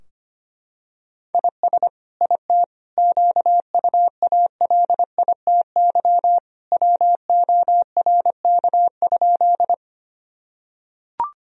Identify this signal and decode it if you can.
Morse code, heard as a keyed tone, about 25 wpm, 700 Hz. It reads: IS IT QUALITY WORK?
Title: Sentences from Top 1000 Words - Review Part 4 - 25wpm